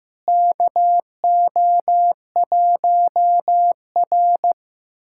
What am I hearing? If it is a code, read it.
Morse code, heard as a keyed tone, about 15 wpm, 700 Hz. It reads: KO1R
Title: USA Callsign DE Callsign - 4 Characters - 15wpm